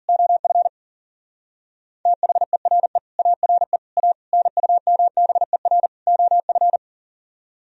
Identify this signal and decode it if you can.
Morse code, heard as a keyed tone, about 53 wpm, 700 Hz. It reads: OF THERE ARE A NUMBER OF